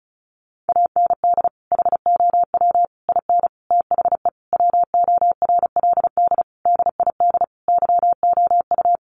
Morse code, heard as a keyed tone, about 34 wpm, 700 Hz. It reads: AND HOW IN THE WORLD DID YOU
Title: Sentences from Top 200 Words 35wpm